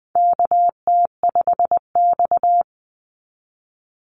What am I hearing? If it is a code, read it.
Morse code, heard as a keyed tone, about 19 wpm, 700 Hz. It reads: KT5X
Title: Call Signs - 20wpm